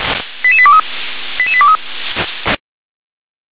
Five-tone selective call (ZVEI), heard as quick runs of five tones, tones of about 70 ms. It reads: 88823 88823
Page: SDR/ZVEI